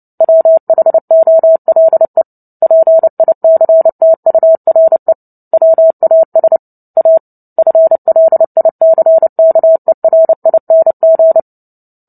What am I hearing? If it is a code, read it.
Morse code, heard as a keyed tone, about 30 wpm, 650 Hz. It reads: WHOLE PICTURE WAS A FLICKERING
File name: Book Two / 03-Chapter 3 - 29 WPM